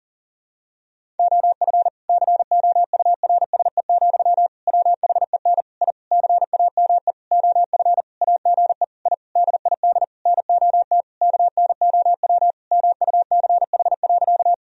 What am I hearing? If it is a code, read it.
Morse code, heard as a keyed tone, about 40 wpm, 700 Hz. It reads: OF COURSE, WHEN I CAME OF AGE I DID NOT KNOW MUCH.
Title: Sentences from Top 1000 Words - Review Part 3 - 40wpm